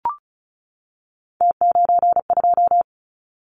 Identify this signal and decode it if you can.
Morse code, heard as a keyed tone, about 36 wpm, 690 Hz. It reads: T92
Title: Callsign Prefixes for Top 50 DXCC - 35wpm